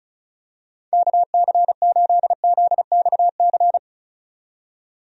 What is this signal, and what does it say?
Morse code, heard as a keyed tone, about 34 wpm, 700 Hz. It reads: KC8ZXC
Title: USA Call Signs - 6 Characters - 35wpm